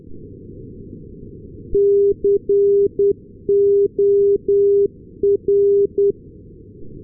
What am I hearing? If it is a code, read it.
Morse code, heard as a keyed tone, about 10 wpm, 400 Hz. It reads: COR
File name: ndb corner cor-388